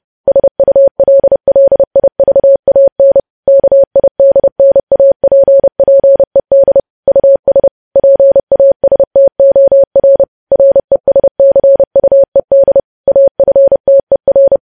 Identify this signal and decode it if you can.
Morse code, heard as a keyed tone, about 30 wpm, 550 Hz. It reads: SULLIVAN KIDNAPPED US PASTOR RESCUED AFTER
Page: news headlines in morse code at 30 wpm